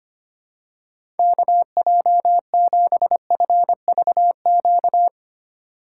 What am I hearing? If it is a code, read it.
Morse code, heard as a keyed tone, about 25 wpm, 700 Hz. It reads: KJ7FVQ
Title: USA Call Signs - 6 Characters - 4x Character Spacing - 25wpm